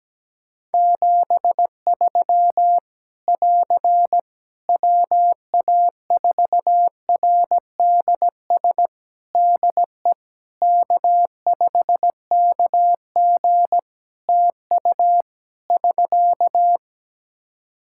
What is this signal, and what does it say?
Morse code, heard as a keyed tone, about 17 wpm, 700 Hz. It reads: 73 <AR> WA4RDS DE K5KG TU <SK>
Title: Practice QSO - Exchange 4 - Ending - 17wpm